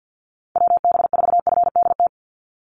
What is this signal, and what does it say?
Morse code, heard as a keyed tone, about 50 wpm, 700 Hz. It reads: WB4FDT
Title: Call Signs Repeated 3x - 50wpm